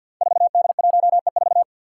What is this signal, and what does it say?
Morse code, heard as a keyed tone, about 52 wpm, 700 Hz. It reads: 4D1E4
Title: Five Challenging Letter-Number Combinations 50wpm